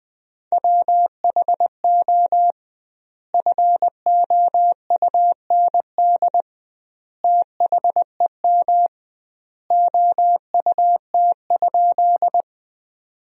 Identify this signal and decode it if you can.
Morse code, heard as a keyed tone, about 20 wpm, 700 Hz. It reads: WHO FOUND THEM OUT?
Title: 2x Word-Spacing - Sentences from Top 300 Words - 20wpm